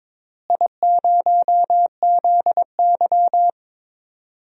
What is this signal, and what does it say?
Morse code, heard as a keyed tone, about 21 wpm, 700 Hz. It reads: I0ZY
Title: DXCC-100 Call Signs - 4 Characters - Repeated 2x - 22wpm